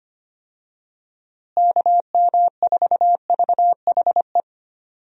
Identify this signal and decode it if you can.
Morse code, heard as a keyed tone, about 25 wpm, 700 Hz. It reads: KM4VHE
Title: USA Call Signs - 6 Characters - Repeated 2x - 25wpm